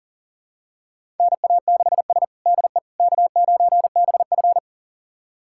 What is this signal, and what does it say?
Morse code, heard as a keyed tone, about 40 wpm, 700 Hz. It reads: NA6S DE K9BF
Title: USA Callsign DE Callsign - 4 Characters - 40wpm